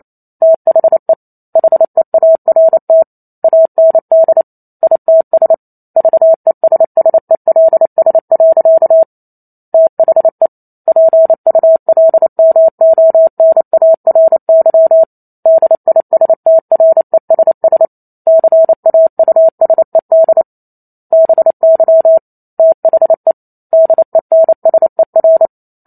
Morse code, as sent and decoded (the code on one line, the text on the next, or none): - .... / .... .- .-. - / .- -. -.. / .. - ... / ...- . ... ... .-.. ... .-.-.- / - .... / .--. ..- .-.. -- --- -. .- .-. -.-- / -.. .. ... - .-. ... ... / -.-. .- ..- ... -.. / -... -.-- / - .... / -.. -. ... .-.
THE HEART AND ITS VESSELS. THE PULMONARY DISTRESS CAUSED BY THE DENSER